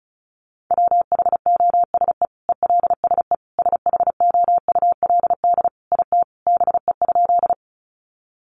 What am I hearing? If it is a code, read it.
Morse code, heard as a keyed tone, about 34 wpm, 700 Hz. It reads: WHOSE ELSE SHOULD IT BE?
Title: Sentences from Top 1200 Words 35wpm